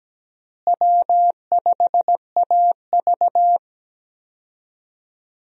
Morse code, heard as a keyed tone, about 17 wpm, 700 Hz. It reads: W5AV
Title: USA Call Signs - 4 Characters - Repeated 3x - 17wpm